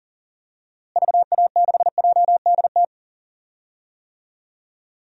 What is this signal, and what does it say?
Morse code, heard as a keyed tone, about 40 wpm, 700 Hz. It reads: VA6JDT
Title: DXCC-100 Call Signs - 6 Characters - Repeated 3x - 40wpm